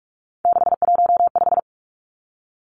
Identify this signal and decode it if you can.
Morse code, heard as a keyed tone, about 44 wpm, 700 Hz. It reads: 615
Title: Numbers - 3-Digit 45wpm